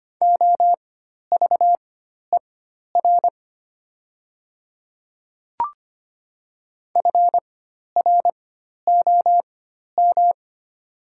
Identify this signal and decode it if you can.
Morse code, heard as a keyed tone, about 25 wpm, spaced as for 12 wpm, 700 Hz. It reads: OVER FROM
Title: 4x Character Spacing - Top 100 Words - 25wpm